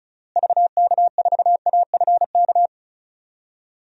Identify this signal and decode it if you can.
Morse code, heard as a keyed tone, about 36 wpm, 700 Hz. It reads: VK4AFK
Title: DXCC-100 Call Signs - 6 Characters - Repeated 3x - 35wpm